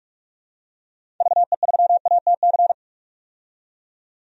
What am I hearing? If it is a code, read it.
Morse code, heard as a keyed tone, about 47 wpm, 700 Hz. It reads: VE3ATC
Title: DXCC-100 Call Signs - 6 Characters - 45wpm